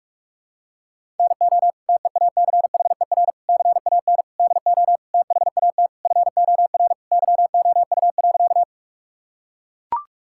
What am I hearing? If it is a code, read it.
Morse code, heard as a keyed tone, about 42 wpm, 700 Hz. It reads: NO TEACHER CAN DO THAT FOR YOU.
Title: Sentences from Top 1000 Words - Review Part 6 - 45wpm